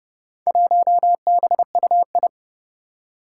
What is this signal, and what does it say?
Morse code, heard as a keyed tone, about 30 wpm, 700 Hz. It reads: 1BUI
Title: Four Challenging Letter-Number Combinations 30wpm